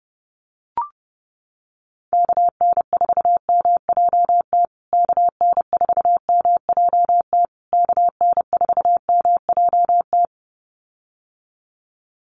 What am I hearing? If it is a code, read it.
Morse code, heard as a keyed tone, about 30 wpm, 700 Hz. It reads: KN4MJT KN4MJT KN4MJT
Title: USA Call Signs Repeated 3x 30wpm